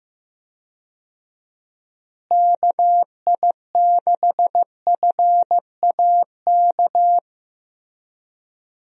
Morse code, heard as a keyed tone, about 15 wpm, 700 Hz. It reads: KI6FAK
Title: USA Call Signs - 6 Characters - Repeated 3x - 15wpm